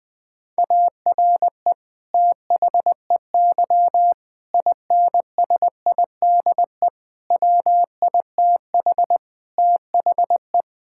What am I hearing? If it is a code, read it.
Morse code, heard as a keyed tone, about 19 wpm, 700 Hz. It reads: ARE THEY INSIDE WITH THE
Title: Sentences from Top 1000 Words - Review Part 6 - 20wpm